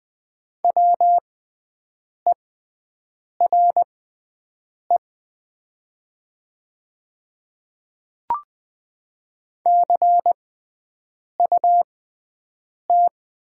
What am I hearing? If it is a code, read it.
Morse code, heard as a keyed tone, about 20 wpm, spaced as for 7 wpm, 700 Hz. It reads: WERE CUT
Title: 6x Character Spacing - Top 500 Words - 20wpm